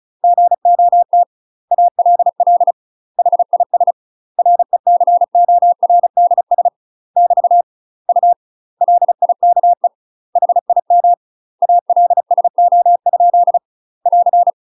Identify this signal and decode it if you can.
Morse code, heard as a keyed tone, about 36 wpm, 700 Hz. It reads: GOT ALL HIS RECORDS <BT> U LIKE HIM ALSO? <AR>
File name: UK MASTER QSOs TRAINING / 35WPM / UK MASTER QSOs 0013